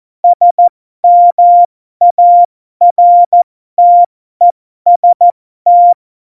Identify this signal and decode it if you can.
Morse code, heard as a keyed tone, about 14 wpm, spaced as for 12 wpm, 700 Hz. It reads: SMARTEST